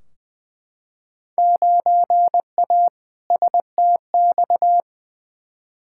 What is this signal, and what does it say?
Morse code, heard as a keyed tone, about 20 wpm, 700 Hz. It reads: 9A STX
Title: Field Day Exchanges - 20wpm